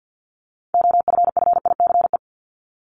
Morse code, heard as a keyed tone, about 50 wpm, 700 Hz. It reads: OFFICE
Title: Top 751-1000 Words 50wpm